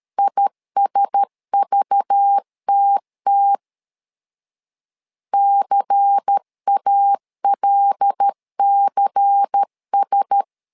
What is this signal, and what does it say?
Morse code, heard as a keyed tone, about 13 wpm, 800 Hz. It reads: ISVTT CALCS
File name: LEZIONI / 5 LBVC / 5 LBVC 12wpm 1955